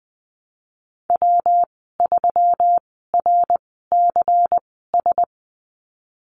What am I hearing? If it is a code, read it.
Morse code, heard as a keyed tone, about 20 wpm, spaced as for 15 wpm, 700 Hz. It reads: W3RCS
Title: 179 Lesson 29g - Slash - Call Signs 20wpm